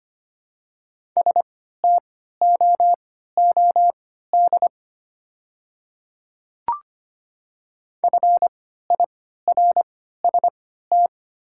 Morse code, heard as a keyed tone, about 25 wpm, spaced as for 14 wpm, 700 Hz. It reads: STOOD FIRST
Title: 3x Character Spacing - Top 500 Words - 25wpm